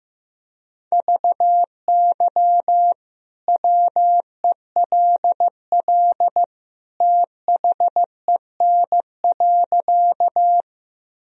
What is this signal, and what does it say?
Morse code, heard as a keyed tone, about 15 wpm, 700 Hz. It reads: VY WELL THEN.
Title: Sentences from Top 200 Words and Abbreviations 15wpm